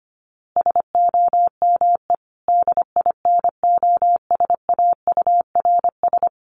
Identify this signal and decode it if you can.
Morse code, heard as a keyed tone, about 25 wpm, 700 Hz. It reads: SOME DINOSAURS
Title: A Sound Of Thunder 25wpm